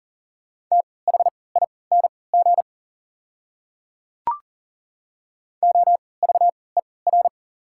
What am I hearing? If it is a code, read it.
Morse code, heard as a keyed tone, about 40 wpm, spaced as for 23 wpm, 690 Hz. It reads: THING OVER
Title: 3x Character Spacing - Top 100 Words - 40wpm